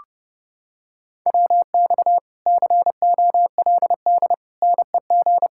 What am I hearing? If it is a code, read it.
Morse code, heard as a keyed tone, about 29 wpm, 700 Hz. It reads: WX COLD NEG